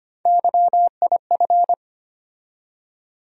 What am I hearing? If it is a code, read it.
Morse code, heard as a keyed tone, about 25 wpm, 700 Hz. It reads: YIF
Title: USA Call Sign Postfixes - 3 Letters 25wpm